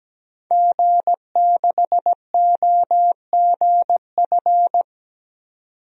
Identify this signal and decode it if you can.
Morse code, heard as a keyed tone, about 17 wpm, 700 Hz. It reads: G6OGF
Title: DXCC-100 Call Signs - 5 Characters - 17wpm